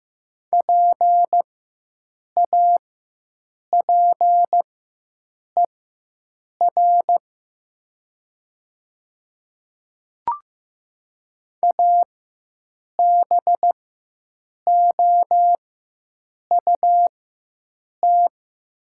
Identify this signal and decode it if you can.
Morse code, heard as a keyed tone, about 15 wpm, spaced as for 7 wpm, 700 Hz. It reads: PAPER ABOUT
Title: 4x Character Spacing - Top 300 Words - 15wpm